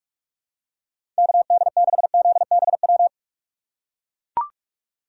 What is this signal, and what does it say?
Morse code, heard as a keyed tone, about 41 wpm, 700 Hz. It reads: KD6ZBW
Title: USA Call Signs - 6 Characters - 2x Character Spacing - 45wpm